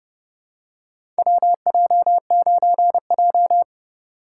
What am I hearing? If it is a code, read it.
Morse code, heard as a keyed tone, about 30 wpm, 700 Hz. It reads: WJ9J
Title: USA Call Signs - 4 Characters - 2x Character Spacing - 30wpm